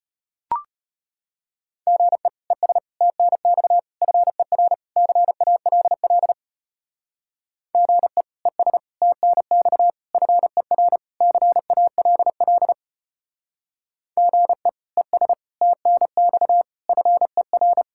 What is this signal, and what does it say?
Morse code, heard as a keyed tone, about 32 wpm, 700 Hz. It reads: GE ES TNX FER CALL GE ES TNX FER CALL GE ES TNX FER